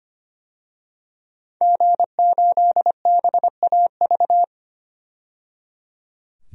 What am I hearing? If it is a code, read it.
Morse code, heard as a keyed tone, about 25 wpm, 690 Hz. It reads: G8BAV
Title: DXCC-100 Call Signs - 5 Characters - 25wpm